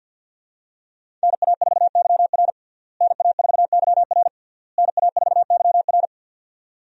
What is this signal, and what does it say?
Morse code, heard as a keyed tone, about 50 wpm, 700 Hz. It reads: NA4YR NA4YR NA4YR
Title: USA Call Signs - 5 Characters - Repeated 3x - 50wpm